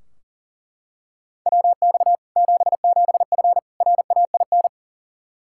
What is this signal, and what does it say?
Morse code, heard as a keyed tone, about 37 wpm, 700 Hz. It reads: WX 77F RAIN
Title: Practice QSO - Exchange 2 - Info - 40wpm